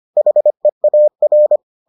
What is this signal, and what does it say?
Morse code, heard as a keyed tone, about 25 wpm, 600 Hz. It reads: HEAR